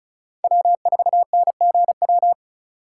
Morse code, heard as a keyed tone, about 34 wpm, 700 Hz. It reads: W4NGW
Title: USA Call Signs - 5 Characters - 3x Character Spacing - 35wpm